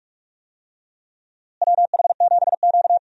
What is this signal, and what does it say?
Morse code, heard as a keyed tone, about 41 wpm, 700 Hz. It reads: WH7Q
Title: USA Call Signs - 4 Characters - Repeated 3x - 45wpm